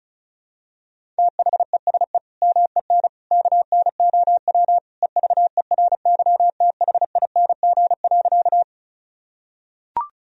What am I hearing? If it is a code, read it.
Morse code, heard as a keyed tone, about 34 wpm, 700 Hz. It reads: THESE MEN KNOW EVERYTHING.
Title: Sentences from Top 1000 Words - Review Part 1 - 35wpm